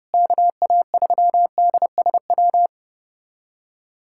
Morse code, heard as a keyed tone, about 29 wpm, 700 Hz. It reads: KA3DSW
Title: USA Callsign DE Callsign - 6 Characters - 30wpm